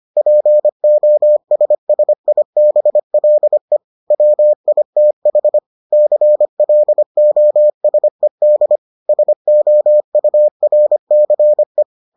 Morse code, heard as a keyed tone, about 25 wpm, 600 Hz. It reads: POSSIBLE WITH CLOSED SOURCE